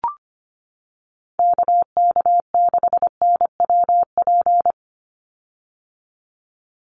Morse code, heard as a keyed tone, about 25 wpm, 700 Hz. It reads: KK6NWP
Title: USA Call Signs 25wpm